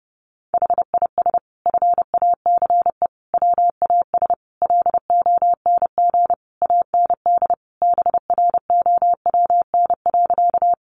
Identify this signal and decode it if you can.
Morse code, heard as a keyed tone, about 30 wpm, 700 Hz. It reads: HIS FACE WAS LONG AND BROWN.
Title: Sentences from Top 1200 Words 30wpm